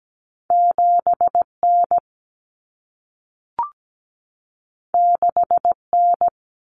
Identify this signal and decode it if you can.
Morse code, heard as a keyed tone, about 17 wpm, 700 Hz. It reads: 7N 6N
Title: Common Cut Numbers - 2-Digit 17wpm